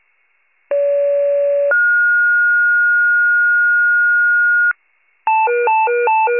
Two-tone page, heard as a one-second tone, then a three-second tone, about 570 then 1430 Hz. It is heard from 0.7 to 4.7 s.